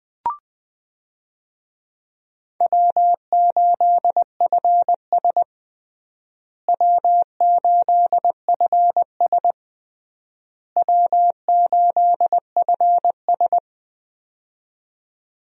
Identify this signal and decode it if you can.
Morse code, heard as a keyed tone, about 21 wpm, 700 Hz. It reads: W8FS W8FS W8FS
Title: USA Call Signs - 4 Characters - Repeated 3x - 20wpm